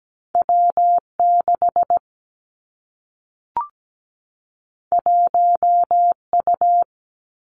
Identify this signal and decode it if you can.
Morse code, heard as a keyed tone, about 17 wpm, 700 Hz. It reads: W 6 1 U
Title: Two Challenging Letter-Number Combinations 17wpm